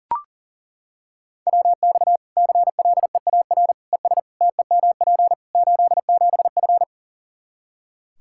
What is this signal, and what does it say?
Morse code, heard as a keyed tone, about 40 wpm, 700 Hz. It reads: WX CLEAR ES TEMP 87F
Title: Practice QSO - Exchange 2 - Info - 40wpm